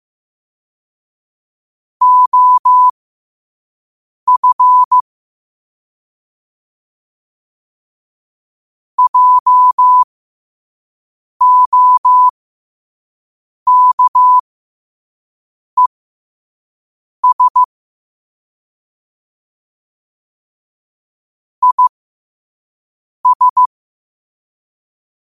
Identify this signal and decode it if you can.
Morse code, heard as a keyed tone, about 15 wpm, spaced as for 5 wpm, 1000 Hz. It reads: OF JOKES IS
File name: qotdfiles / 5wpm